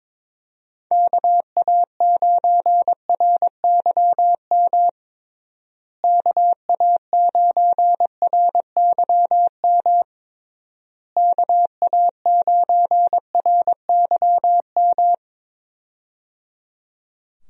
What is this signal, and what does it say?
Morse code, heard as a keyed tone, about 22 wpm, 700 Hz. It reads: KA9RYM KA9RYM KA9RYM
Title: USA Call Signs - 6 Characters - Repeated 3x - 22wpm